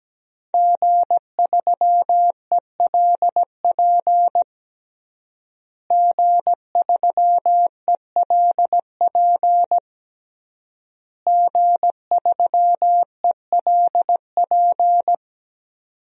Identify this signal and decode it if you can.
Morse code, heard as a keyed tone, about 17 wpm, 700 Hz. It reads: G3ELP G3ELP G3ELP
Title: DXCC-100 Call Signs - 5 Characters - Repeated 3x - 17wpm